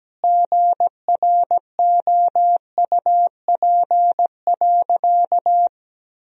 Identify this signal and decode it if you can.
Morse code, heard as a keyed tone, about 17 wpm, 700 Hz. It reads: GROUP.
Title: Sentences from Top 1000 Words - Review Part 1 - 17wpm